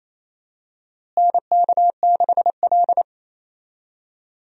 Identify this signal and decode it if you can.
Morse code, heard as a keyed tone, about 28 wpm, 700 Hz. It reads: NK6L
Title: USA Call Signs - 4 Characters - Repeated 2x - 28wpm